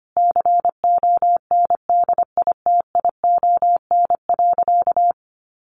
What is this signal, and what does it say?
Morse code, heard as a keyed tone, about 25 wpm, 700 Hz. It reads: CONDITION.
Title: Sentences from Top 1100 Words 25wpm